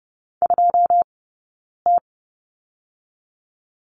Morse code, heard as a keyed tone, about 30 wpm, 700 Hz. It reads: 2 T
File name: course / 30:10wpm / 086 Lesson 15d - Number 5 - Two Characters 30wpm